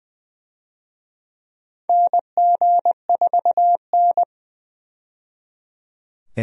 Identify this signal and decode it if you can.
Morse code, heard as a keyed tone, about 20 wpm, 700 Hz. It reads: NG4N
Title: USA Call Signs - 4 Characters - 20wpm